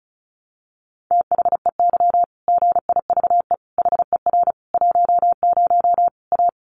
Morse code, heard as a keyed tone, about 34 wpm, 700 Hz. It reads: THEY GIVE HER 10 A
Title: Easy Phrases 35wpm